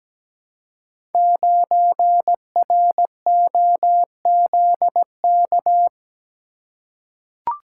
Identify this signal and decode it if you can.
Morse code, heard as a keyed tone, about 17 wpm, 700 Hz. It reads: 9ROZK
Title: Code Groups - 5 Letter:Numbers - 17wpm